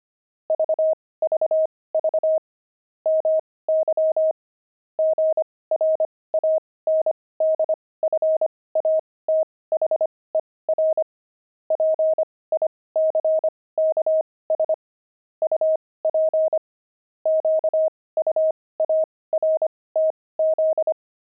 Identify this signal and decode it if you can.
Morse code, heard as a keyed tone, about 25 wpm, spaced as for 18 wpm, 630 Hz. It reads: VVV MY GRANDFATHER PICKS UP QUARTZ
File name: pangram-fs-18-cs-25-grandpa